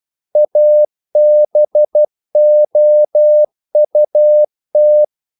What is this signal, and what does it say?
Morse code, heard as a keyed tone, about 12 wpm, 600 Hz. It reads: ABOUT
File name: CW / Practice / EnglishQuotations0112